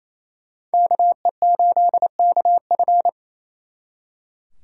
Morse code, heard as a keyed tone, about 34 wpm, 700 Hz. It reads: KE8KF
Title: Speed Racing - USA 2x2 Call Signs 28wpm